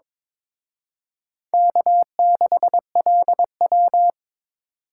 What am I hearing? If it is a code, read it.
Morse code, heard as a keyed tone, about 26 wpm, 700 Hz. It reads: K6LW
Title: Speed Racing - USA 1x2 Call Signs 22wpm